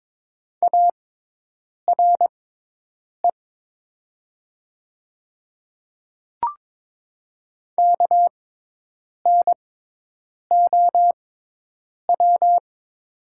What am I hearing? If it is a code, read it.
Morse code, heard as a keyed tone, about 22 wpm, spaced as for 8 wpm, 700 Hz. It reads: ARE KNOW
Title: 6x Character Spacing - Top 100 Words - 22wpm